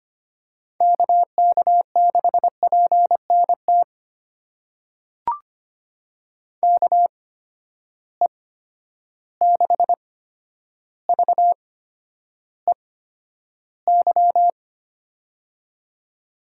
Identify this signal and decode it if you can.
Morse code, heard as a keyed tone, about 25 wpm, spaced as for 11 wpm, 700 Hz. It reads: KK6PNT KE6VEY
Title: USA Call Signs - 6 Characters - 8x Character Spacing - 25wpm